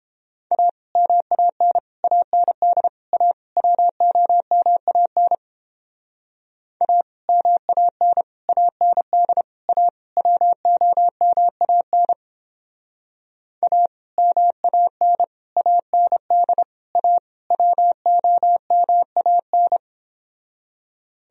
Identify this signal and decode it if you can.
Morse code, heard as a keyed tone, about 31 wpm, 700 Hz. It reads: A MAN AND A WOMAN A MAN AND A WOMAN A MAN AND A WOMAN